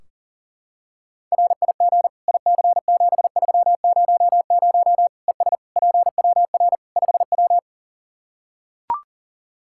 Morse code, heard as a keyed tone, about 37 wpm, 700 Hz. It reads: RIG IC7300 ES PWR 5W
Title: Practice QSO - Exchange 2 - Info - 40wpm